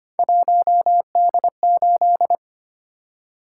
Morse code, heard as a keyed tone, about 29 wpm, 700 Hz. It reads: 1D8
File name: Speed-Racing - Three Letter-Number Combinations - 25wpm